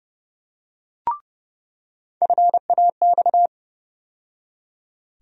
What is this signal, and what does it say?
Morse code, heard as a keyed tone, about 32 wpm, 700 Hz. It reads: FAX
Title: USA Call Sign Postfixes - 3 Letters 30wpm